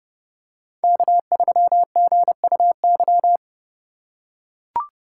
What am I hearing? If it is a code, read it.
Morse code, heard as a keyed tone, about 30 wpm, 700 Hz. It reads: K3GUY
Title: USA Call Signs - 2x Character Spacing - 30wpm